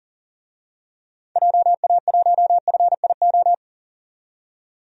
Morse code, heard as a keyed tone, about 40 wpm, 700 Hz. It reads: JA1FIO
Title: DXCC-100 Call Signs - 6 Characters - 40wpm